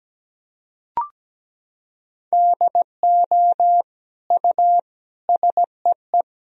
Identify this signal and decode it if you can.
Morse code, heard as a keyed tone, about 17 wpm, 700 Hz. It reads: DO U SEE
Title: Sentences from Top 500 Words and Abbreviations 17wpm